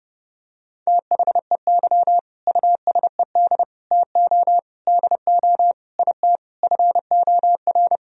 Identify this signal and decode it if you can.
Morse code, heard as a keyed tone, about 30 wpm, 700 Hz. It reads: THEY USED TO DO IT FOR
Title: Sentences from Top 1000 Words - Review Part 4 - 30wpm